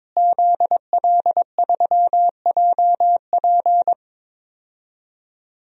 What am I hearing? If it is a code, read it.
Morse code, heard as a keyed tone, about 22 wpm, 700 Hz. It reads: ZL3JP
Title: DXCC-100 Call Signs - 5 Characters - Repeated 2x - 22wpm